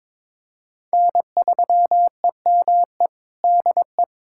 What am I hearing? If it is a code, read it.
Morse code, heard as a keyed tone, about 22 wpm, 700 Hz. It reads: N3EME DE
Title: USA Callsign DE Callsign - 5 Characters - 22wpm